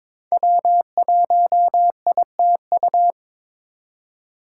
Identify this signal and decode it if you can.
Morse code, heard as a keyed tone, about 22 wpm, 700 Hz. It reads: W1ITU
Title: USA Call Signs - 4x Character Spacing - 22wpm